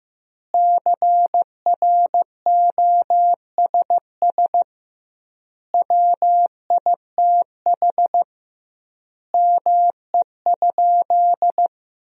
Morse code, heard as a keyed tone, about 16 wpm, 700 Hz. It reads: CROSS WITH ME?
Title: 2x Word-Spacing - Sentences from Top 300 Words - 15wpm